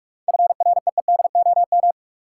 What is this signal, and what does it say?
Morse code, heard as a keyed tone, about 46 wpm, 700 Hz. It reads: FREEDOM
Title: Sentences from Top 800 Words - New Word Drill - 45wpm